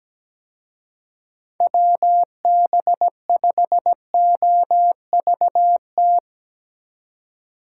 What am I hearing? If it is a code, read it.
Morse code, heard as a keyed tone, about 20 wpm, 700 Hz. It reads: WB5OVT
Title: Speed Racing - USA Call Signs 17wpm